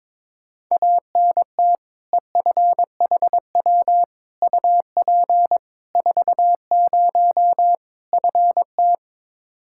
Morse code, heard as a keyed tone, about 22 wpm, 700 Hz. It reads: ANT EFHW UP 40 FT
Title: K1USN SST QSO Practice 22wpm